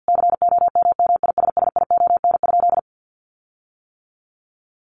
Morse code, heard as a keyed tone, about 50 wpm, 700 Hz. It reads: COMMISSION?
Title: Sentences from Top 1400 Words 50wpm